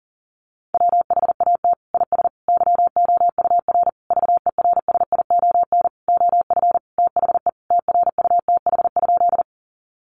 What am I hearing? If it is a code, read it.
Morse code, heard as a keyed tone, about 40 wpm, 700 Hz. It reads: WHAT IS YOUR VERSION OF THE TRUTH?
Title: Sentences from Top 1400 Words 40wpm